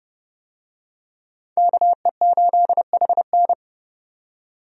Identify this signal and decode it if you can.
Morse code, heard as a keyed tone, about 30 wpm, 700 Hz. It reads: KE8HN
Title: USA Call Signs - 5 Characters - 30wpm